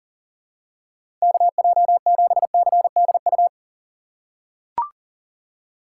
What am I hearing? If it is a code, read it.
Morse code, heard as a keyed tone, about 40 wpm, 700 Hz. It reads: KJ7CDU
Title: USA Call Signs - 6 Characters - 6x Character Spacing - 40wpm